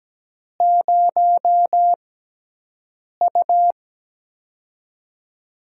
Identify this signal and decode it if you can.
Morse code, heard as a keyed tone, about 17 wpm, spaced as for 6 wpm, 700 Hz. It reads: 0U
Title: USA Call Signs - 6x Character Spacing - 17wpm